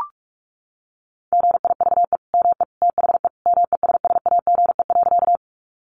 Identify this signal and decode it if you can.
Morse code, heard as a keyed tone, about 44 wpm, 700 Hz. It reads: GIVE ME THE MESSAGE.
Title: Sentences from Top 1100 Words 45wpm